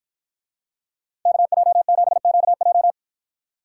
Morse code, heard as a keyed tone, about 50 wpm, 700 Hz. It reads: KJ7QJ